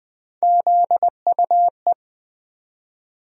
Morse code, heard as a keyed tone, about 20 wpm, 700 Hz. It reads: ZUE
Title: USA Call Sign Postfixes - 3 Letters 20wpm